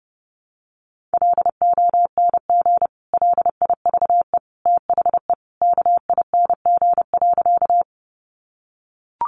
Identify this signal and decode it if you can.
Morse code, heard as a keyed tone, about 30 wpm, 700 Hz. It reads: LONG LIVE THE KING.
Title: Sentences from Top 400 Words 30wpm